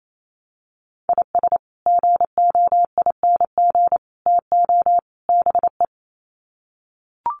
Morse code, heard as a keyed tone, about 28 wpm, 700 Hz. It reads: IS GOING TO BE
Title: Sets of 4 Words 28wpm